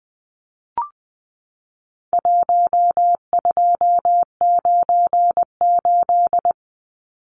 Morse code, heard as a keyed tone, about 20 wpm, 700 Hz. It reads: 1298
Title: Numbers - 4-Digit 20wpm